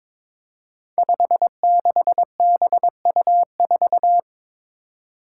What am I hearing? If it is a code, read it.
Morse code, heard as a keyed tone, about 23 wpm, 700 Hz. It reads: 56BU4
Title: Five Challenging Letter-Number Combinations 22wpm